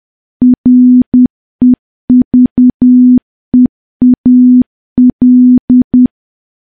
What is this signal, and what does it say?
Morse code, heard as a keyed tone, about 10 wpm, 250 Hz. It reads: REVEAL
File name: morse / morse-training-10WPM